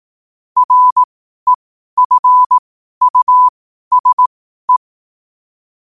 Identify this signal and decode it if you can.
Morse code, heard as a keyed tone, about 18 wpm, spaced as for 12 wpm, 1000 Hz. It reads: REFUSE
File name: qotdfiles / 12wpm